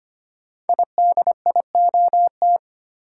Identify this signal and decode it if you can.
Morse code, heard as a keyed tone, about 25 wpm, 700 Hz. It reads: IDIOT
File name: 5 Letter Words - First 3 Letters Given - 25wpm